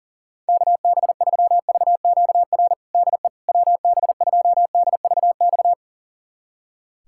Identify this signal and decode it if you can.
Morse code, heard as a keyed tone, about 40 wpm, 700 Hz. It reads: KB3VQR DE WB2DVX
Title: USA Callsign DE Callsign - 6 Characters - 40wpm